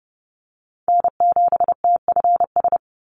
Morse code, heard as a keyed tone, about 30 wpm, 700 Hz. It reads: N7TFS
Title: USA Call Signs 30wpm